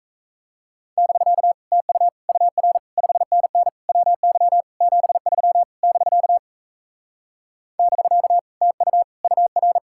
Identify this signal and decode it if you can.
Morse code, heard as a keyed tone, about 37 wpm, 700 Hz. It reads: <BK> TU UR 5NN WY 73 <BK> <BK> TU UR